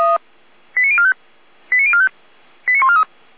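Five-tone selective call (ZVEI), heard as quick runs of five tones, tones of about 70 ms. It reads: 89045 89045 89143
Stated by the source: ZVEI five-tone selective call received with an SDR, recorded on the Feuerwehr Orlamünde audio server.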